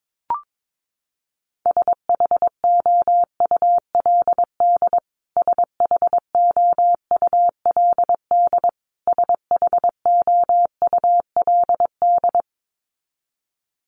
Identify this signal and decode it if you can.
Morse code, heard as a keyed tone, about 22 wpm, 700 Hz. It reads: SHOULD SHOULD SHOULD
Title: Top 300 Words Repeated 22wpm